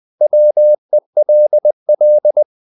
Morse code, heard as a keyed tone, about 20 wpm, 600 Hz. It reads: WELL
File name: CW / Practice / 500MostCommonEnglishWords20